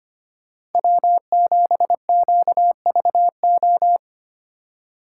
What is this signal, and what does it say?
Morse code, heard as a keyed tone, about 32 wpm, 700 Hz. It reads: W7QVO